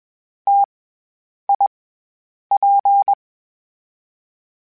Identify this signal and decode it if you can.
Morse code, heard as a keyed tone, about 21 wpm, 800 Hz. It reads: T I P
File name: Variable Spacing / Variable Spacing-mp3s / T AOEs @20